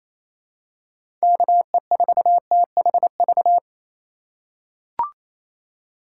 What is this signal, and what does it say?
Morse code, heard as a keyed tone, about 28 wpm, 700 Hz. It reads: KE4THV
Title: USA Call Signs - 6 Characters - 3x Character Spacing - 28wpm